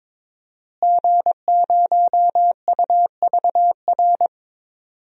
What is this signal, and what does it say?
Morse code, heard as a keyed tone, about 22 wpm, 700 Hz. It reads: G0UVR
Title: DXCC-100 Call Signs - 5 Characters - Repeated 3x - 22wpm